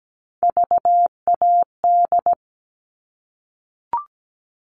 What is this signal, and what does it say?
Morse code, heard as a keyed tone, about 17 wpm, 700 Hz. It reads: VAD
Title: All Cut Numbers - 3-Digit 17wpm